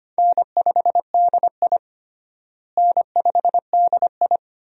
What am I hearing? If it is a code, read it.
Morse code, heard as a keyed tone, about 25 wpm, 700 Hz. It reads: N5DI N5DI
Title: USA Call Signs - 4 Characters - Repeated 2x - 25wpm